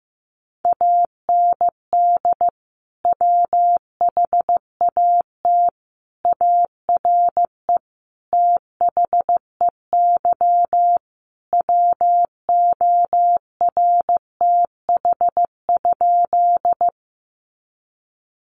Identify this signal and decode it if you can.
Morse code, heard as a keyed tone, about 15 wpm, 700 Hz. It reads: AND WHAT ARE THEY WORTH?
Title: Sentences from Top 800 Words 15wpm